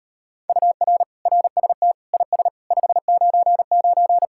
Morse code, heard as a keyed tone, about 38 wpm, 700 Hz. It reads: UR RST IS 599